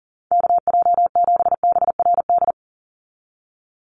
Morse code, heard as a keyed tone, about 40 wpm, 700 Hz. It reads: KJ7BRD
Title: USA Call Signs 40wpm